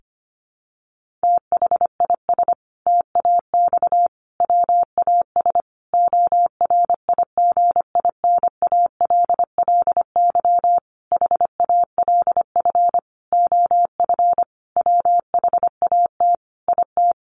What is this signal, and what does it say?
Morse code, heard as a keyed tone, about 25 wpm, 700 Hz. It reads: THIS TAX WAS ORIGINALLY HALF OF WHAT IT
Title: Sentences from Top 1500 Words 25wpm